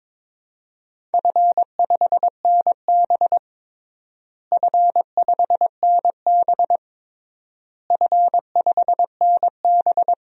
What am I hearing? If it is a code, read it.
Morse code, heard as a keyed tone, about 22 wpm, 700 Hz. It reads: F5NB F5NB F5NB
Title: DXCC-100 Call Signs - 4 Characters - Repeated 3x - 22wpm